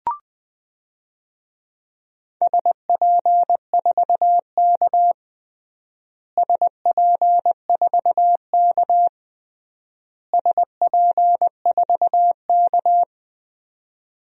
Morse code, heard as a keyed tone, about 21 wpm, 700 Hz. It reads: SP4K SP4K SP4K
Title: DXCC-100 Call Signs - 4 Characters - Repeated 3x - 20wpm